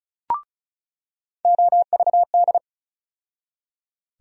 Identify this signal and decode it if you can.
Morse code, heard as a keyed tone, about 36 wpm, 700 Hz. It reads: OVD
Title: USA Call Sign Postfixes - 3 Letters 35wpm